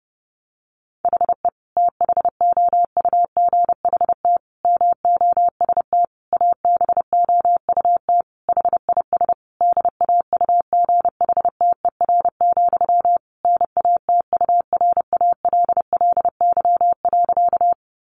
Morse code, heard as a keyed tone, about 30 wpm, 700 Hz. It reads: HE THOUGHT MOST ABOUT HIS DAUGHTER, NATURALLY.
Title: Sentences from Top 1500 Words 30wpm